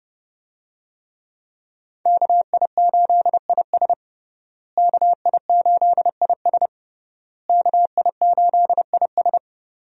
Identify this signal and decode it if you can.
Morse code, heard as a keyed tone, about 30 wpm, 700 Hz. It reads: KI8IS KI8IS KI8IS
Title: USA Call Signs - 5 Characters - Repeated 3x - 30wpm